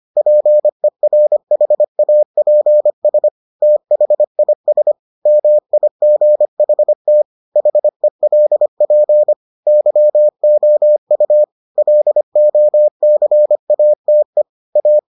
Morse code, heard as a keyed tone, about 25 wpm, 600 Hz. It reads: PERHAPS THIS MIGHT HELP YOU LOCATE A